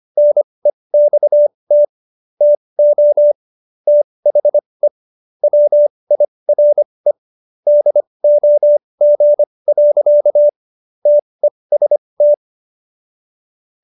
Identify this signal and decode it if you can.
Morse code, heard as a keyed tone, about 25 wpm, spaced as for 20 wpm, 590 Hz. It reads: NEXT TO THE WIRE DOG. TEST